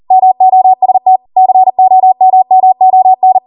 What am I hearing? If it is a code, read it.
Morse code, heard as a keyed tone, about 40 wpm, 750 Hz. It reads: MOST COMMON